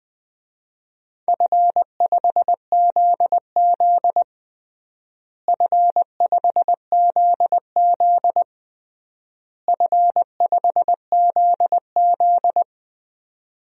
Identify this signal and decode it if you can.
Morse code, heard as a keyed tone, about 21 wpm, 700 Hz. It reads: F5ZZ F5ZZ F5ZZ
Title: DXCC-100 Call Signs - 4 Characters - Repeated 3x - 20wpm